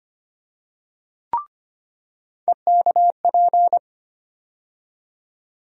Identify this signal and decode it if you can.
Morse code, heard as a keyed tone, about 25 wpm, 700 Hz. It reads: EKP
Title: USA Call Sign Postfixes - 3 Letters 25wpm